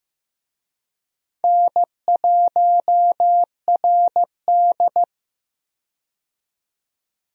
Morse code, heard as a keyed tone, about 16 wpm, 700 Hz. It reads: N1RD
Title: USA Call Signs - 4 Characters - 15wpm